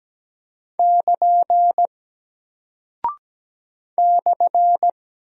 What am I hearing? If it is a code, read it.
Morse code, heard as a keyed tone, about 17 wpm, 700 Hz. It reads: <KN> /
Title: Single Character (Letter:Number:Punctuation:Prosign) Alt 17wpm